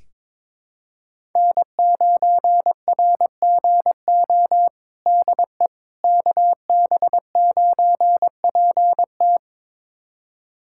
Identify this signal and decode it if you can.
Morse code, heard as a keyed tone, about 21 wpm, 700 Hz. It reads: N9RGO DE KB9PT
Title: USA Callsign DE Callsign - 5 Characters - 22wpm